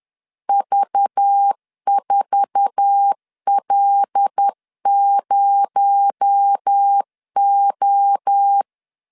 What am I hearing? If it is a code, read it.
Morse code, heard as a keyed tone, about 10 wpm, 800 Hz. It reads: V4L0O
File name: CW / DET / 1 9 1500C S B 10wpm 4439